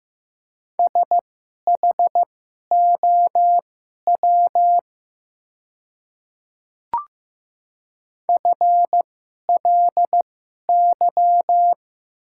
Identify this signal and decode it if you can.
Morse code, heard as a keyed tone, about 15 wpm, spaced as for 11 wpm, 700 Hz. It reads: SHOW FLY
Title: 2x Character Spacing - Top 500 Words - 15wpm